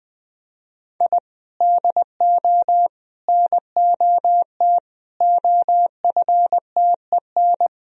Morse code, heard as a keyed tone, about 21 wpm, 700 Hz. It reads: I DO NOT OFTEN
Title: Sentences from Top 1000 Words - Review Part 1 - 20wpm